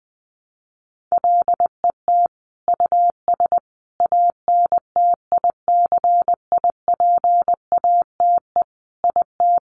Morse code, heard as a keyed tone, about 20 wpm, 700 Hz. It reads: LET US ANTICIPATE IT
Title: Sentences from Top 1500 Words 20wpm